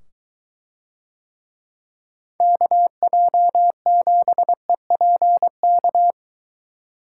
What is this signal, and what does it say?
Morse code, heard as a keyed tone, about 23 wpm, 700 Hz. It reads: KJ7EPK